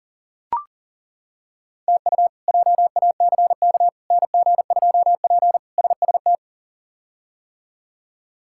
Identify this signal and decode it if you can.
Morse code, heard as a keyed tone, about 40 wpm, 700 Hz. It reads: TU JACK NG2P SST
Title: K1USN SST QSO Practice - 40wpm